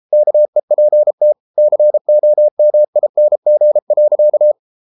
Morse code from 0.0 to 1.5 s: KEPT

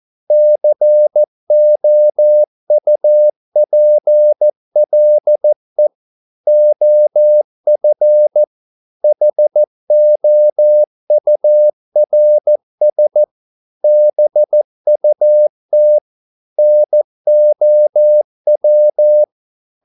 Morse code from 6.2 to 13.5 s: OF HOURS

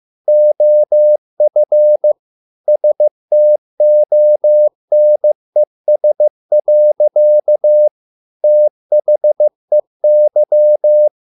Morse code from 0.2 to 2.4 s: OF

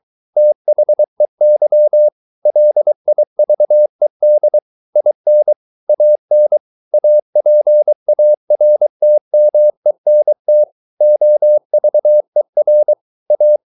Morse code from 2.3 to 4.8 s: LIVED

